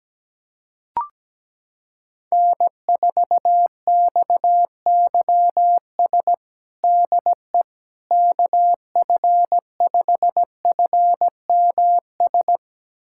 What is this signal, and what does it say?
Morse code, heard as a keyed tone, about 17 wpm, 700 Hz. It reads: N4XYS DE KF5FMS
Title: USA Callsign DE Callsign 17wpm